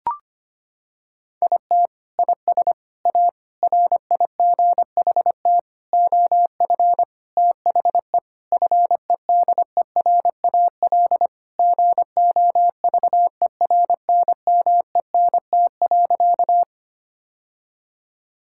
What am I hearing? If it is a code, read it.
Morse code, heard as a keyed tone, about 25 wpm, 700 Hz. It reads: IT IS A RIGHT OF THE FEDERAL GOVERNMENT.
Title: Sentences from Top 1000 Words - Review Part 5 - 25wpm